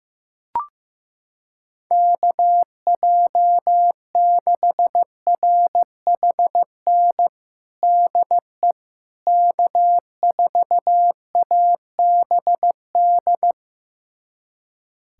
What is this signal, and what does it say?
Morse code, heard as a keyed tone, about 15 wpm, 700 Hz. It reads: KJ6RHN DE K4ABD
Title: USA Callsign DE Callsign 15wpm